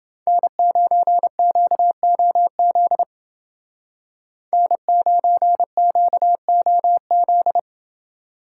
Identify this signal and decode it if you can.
Morse code, heard as a keyed tone, about 29 wpm, 700 Hz. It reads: N9QOZ N9QOZ